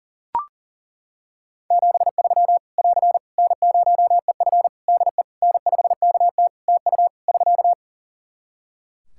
Morse code, heard as a keyed tone, about 40 wpm, 700 Hz. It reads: 73 <AR> N0EF DE N5KT TU <SK>
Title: Practice QSO - Exchange 4 - Ending - 40wpm